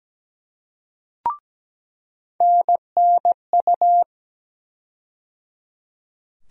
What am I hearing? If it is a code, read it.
Morse code, heard as a keyed tone, about 17 wpm, 700 Hz. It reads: NNU
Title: USA Call Sign Postfixes - 3 Letters 17wpm